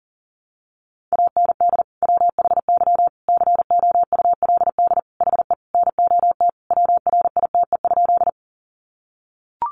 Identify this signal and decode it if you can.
Morse code, heard as a keyed tone, about 40 wpm, 700 Hz. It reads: AND WHY COULD HE NOT WRITE?
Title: Sentences from Top 600 Words 40wpm